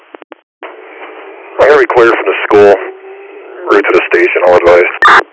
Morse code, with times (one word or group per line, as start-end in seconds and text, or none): none